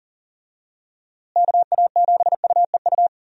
Morse code, heard as a keyed tone, about 37 wpm, 700 Hz. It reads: KA7UEU
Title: USA Call Signs - 6 Characters - Repeated 2x - 40wpm